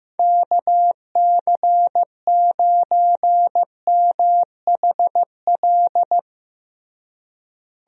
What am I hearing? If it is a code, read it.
Morse code, heard as a keyed tone, about 15 wpm, 700 Hz. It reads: KC9MHL
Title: USA Call Signs - 6 Characters - Repeated 2x - 15wpm